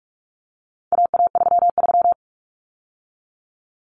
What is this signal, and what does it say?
Morse code, heard as a keyed tone, about 44 wpm, 700 Hz. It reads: AA33
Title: Common Cut Numbers - 4-Digit 45wpm